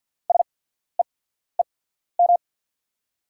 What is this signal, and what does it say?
Morse code, heard as a keyed tone, about 46 wpm, spaced as for 14 wpm, 690 Hz. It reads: SEEM
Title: 8x Character Spacing - Top 500 Words - 50wpm